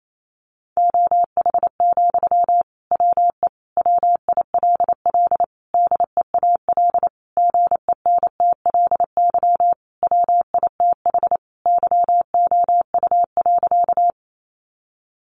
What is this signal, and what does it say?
Morse code, heard as a keyed tone, about 28 wpm, 700 Hz. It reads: OH, WE WILL DEAL GENTLY WITH YOU.
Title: Sentences from Top 1500 Words 28wpm